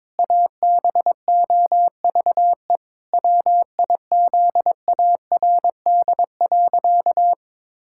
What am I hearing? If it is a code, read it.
Morse code, heard as a keyed tone, about 21 wpm, 700 Hz. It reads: ABOVE WIZARD.
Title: Pangram Challenge v2 - 22wpm